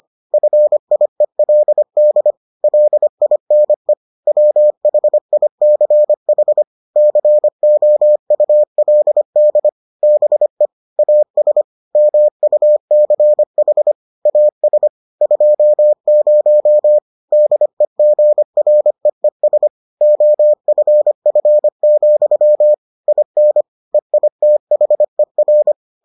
Morse code, heard as a keyed tone, about 25 wpm, 600 Hz. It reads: FIELD LINE WHICH COULD BE AS MUCH AS 20 DEGREES OFF, IN EITHER